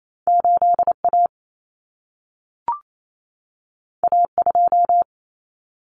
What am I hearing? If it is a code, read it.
Morse code, heard as a keyed tone, about 28 wpm, 700 Hz. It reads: 8A A2
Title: Common Cut Numbers - 2-Digit 28wpm